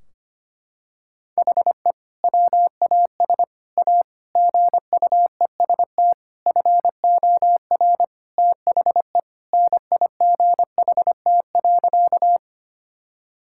Morse code, heard as a keyed tone, about 25 wpm, 700 Hz. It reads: HE WAS A GUEST FOR THE NIGHT.
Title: Sentences from Top 1600 Words 25wpm